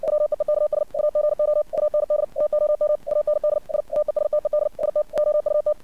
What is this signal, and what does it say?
Morse code, heard as a keyed tone, about 57 wpm, 600 Hz. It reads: WEEKS AGO AND TOM AND I TESTED IT OUT
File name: archived / CRSnet